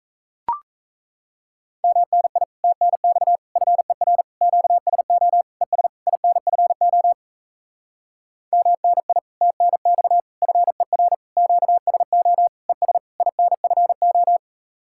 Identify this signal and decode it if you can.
Morse code, heard as a keyed tone, about 37 wpm, 700 Hz. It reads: MNI TNX FER QSO ES INFO MNI TNX FER QSO ES INFO